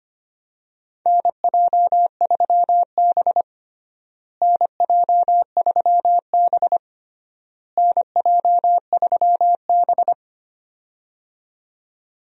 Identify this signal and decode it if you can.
Morse code, heard as a keyed tone, about 25 wpm, 700 Hz. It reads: NJ3B NJ3B NJ3B
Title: USA Call Signs - 4 Characters - Repeated 3x - 25wpm